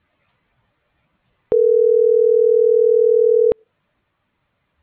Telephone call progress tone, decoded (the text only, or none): ringback tone